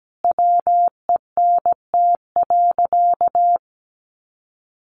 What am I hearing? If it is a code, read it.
Morse code, heard as a keyed tone, about 17 wpm, 700 Hz. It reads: WENT.
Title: Sentences from Top 1400 Words 17wpm